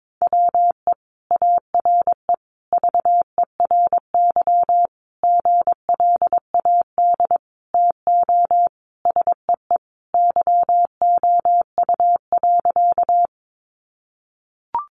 Morse code, heard as a keyed tone, about 22 wpm, 700 Hz. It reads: WE ARE VERY GLAD TO SEE YOU.
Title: Sentences from Top 800 Words 22wpm